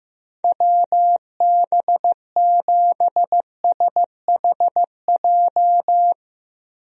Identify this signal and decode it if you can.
Morse code, heard as a keyed tone, about 15 wpm, 700 Hz. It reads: WB7SHJ
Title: USA Call Signs - 6 Characters - 15wpm